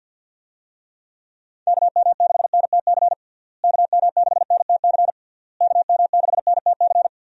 Morse code, heard as a keyed tone, about 50 wpm, 700 Hz. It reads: KM6NTC KM6NTC KM6NTC
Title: USA Call Signs - 6 Characters - Repeated 3x - 50wpm